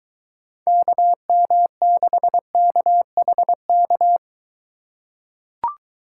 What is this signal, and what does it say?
Morse code, heard as a keyed tone, about 22 wpm, 700 Hz. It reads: KM6KHK